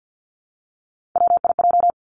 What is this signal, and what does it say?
Morse code, heard as a keyed tone, about 50 wpm, 700 Hz. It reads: WIJ
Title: Three Challenging Letter-Number Combinations 50wpm